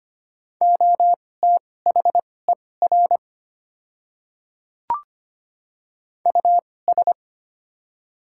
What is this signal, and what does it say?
Morse code, heard as a keyed tone, about 25 wpm, spaced as for 18 wpm, 700 Hz. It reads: OTHER US
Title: 2x Character Spacing - Top 300 Words - 25wpm